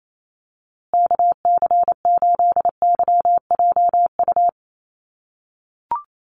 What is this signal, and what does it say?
Morse code, heard as a keyed tone, about 28 wpm, 700 Hz. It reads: KC8YJU
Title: USA Call Signs Repeated 3x 28wpm